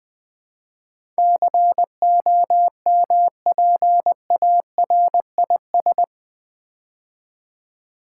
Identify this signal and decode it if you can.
Morse code, heard as a keyed tone, about 20 wpm, 700 Hz. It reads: COMPARIS
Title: Build A Word - 4 Letters At A Time - 10-Letter Words - 20wpm